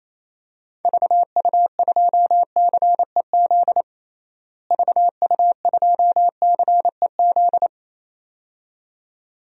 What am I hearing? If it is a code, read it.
Morse code, heard as a keyed tone, about 28 wpm, 700 Hz. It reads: VU2CEZ VU2CEZ
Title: DXCC-100 Call Signs - 6 Characters - Repeated 2x - 28wpm